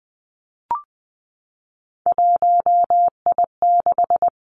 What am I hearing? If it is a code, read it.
Morse code, heard as a keyed tone, about 20 wpm, 700 Hz. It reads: 1I6
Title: Three Challenging Letter-Number Combinations 20wpm